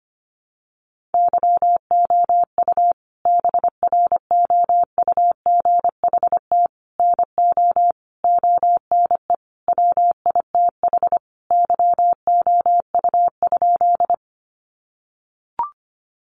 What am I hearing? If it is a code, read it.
Morse code, heard as a keyed tone, about 25 wpm, 700 Hz. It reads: YOU BROUGHT NO ONE WITH YOU?
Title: Sentences from Top 500 Words 25wpm